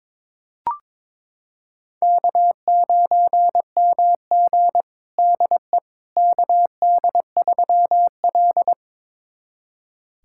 Morse code, heard as a keyed tone, about 22 wpm, 700 Hz. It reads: K9MG DE KD3L
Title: USA Callsign DE Callsign - 4 Characters - 22wpm